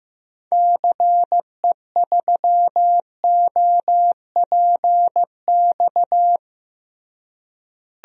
Morse code, heard as a keyed tone, about 15 wpm, 700 Hz. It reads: CE3OPX
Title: DXCC-100 Call Signs - 6 Characters - Repeated 2x - 15wpm